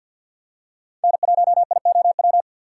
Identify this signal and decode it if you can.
Morse code, heard as a keyed tone, about 51 wpm, 690 Hz. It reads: N1IOW
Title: USA Call Signs - 5 Characters - 6x Character Spacing - 50wpm